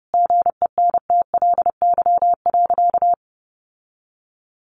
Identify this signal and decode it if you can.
Morse code, heard as a keyed tone, about 30 wpm, 700 Hz. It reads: GENTLY.
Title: Sentences from Top 1500 Words 30wpm